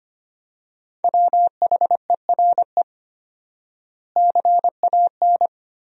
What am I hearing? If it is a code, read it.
Morse code, heard as a keyed tone, about 25 wpm, 700 Hz. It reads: WHERE CAN
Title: 4x Word-Spacing - Sentences from Top 500 Words - 25wpm